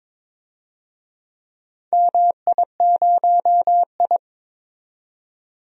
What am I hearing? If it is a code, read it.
Morse code, heard as a keyed tone, about 22 wpm, 700 Hz. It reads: MI0I
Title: DXCC-100 Call Signs - 4 Characters - 22wpm